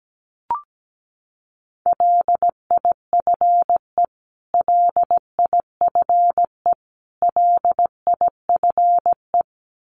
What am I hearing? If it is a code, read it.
Morse code, heard as a keyed tone, about 17 wpm, 700 Hz. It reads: LIFE LIFE LIFE
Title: Top 300 Words Repeated 17wpm